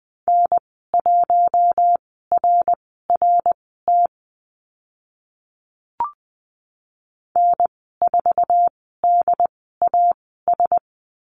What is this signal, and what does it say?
Morse code, heard as a keyed tone, about 20 wpm, spaced as for 15 wpm, 700 Hz. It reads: N1RRT N4DAS
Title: 069 Lesson 12g - Letter d - Call Signs 20wpm